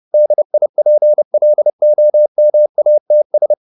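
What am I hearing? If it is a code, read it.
Morse code, heard as a keyed tone, about 30 wpm, 600 Hz. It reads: DIPLOMATS
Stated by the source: Morse code at 30 wpm, the novel Never by Ken Follett in English, from CW med Gnister.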